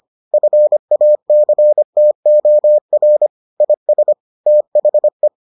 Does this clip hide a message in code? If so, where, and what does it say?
Yes: Morse code from 0.3 to 5.3 s, heard as a keyed tone, about 25 wpm, 600 Hz. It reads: FACTOR IS THE